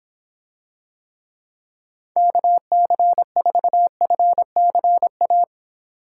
Morse code, heard as a keyed tone, about 26 wpm, 700 Hz. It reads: KC4FCA